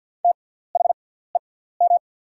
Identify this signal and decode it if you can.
Morse code, heard as a keyed tone, about 48 wpm, spaced as for 17 wpm, 700 Hz. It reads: THEM
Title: 6x Character Spacing - Top 100 Words - 50wpm